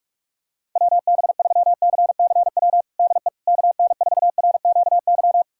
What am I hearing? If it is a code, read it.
Morse code, heard as a keyed tone, about 45 wpm, 700 Hz. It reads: WB3CCW DE KN4RQY
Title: USA Callsign DE Callsign - 6 Characters - 45wpm